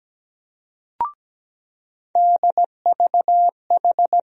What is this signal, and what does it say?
Morse code, heard as a keyed tone, about 17 wpm, 700 Hz. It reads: DVH
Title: USA Call Sign Postfixes - 3 Letters 17wpm